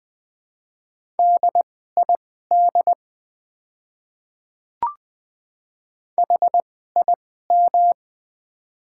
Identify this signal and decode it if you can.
Morse code, heard as a keyed tone, about 20 wpm, spaced as for 14 wpm, 700 Hz. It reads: DID HIM
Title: 2x Character Spacing - Top 500 Words - 20wpm